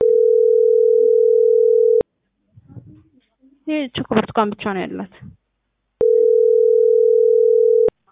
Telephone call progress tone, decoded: ringback tone

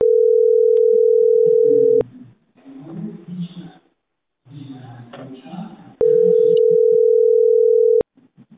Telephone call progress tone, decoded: ringback tone

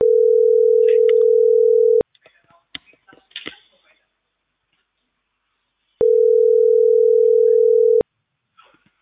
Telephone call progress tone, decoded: ringback tone

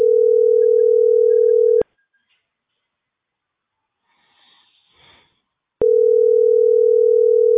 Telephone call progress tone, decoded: ringback tone

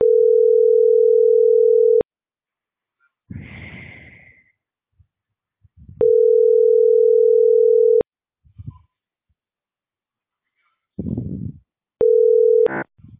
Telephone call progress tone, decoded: ringback tone

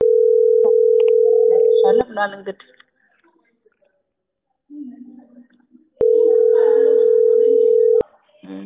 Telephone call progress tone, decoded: ringback tone

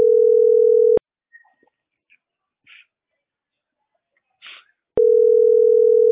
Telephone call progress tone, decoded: ringback tone

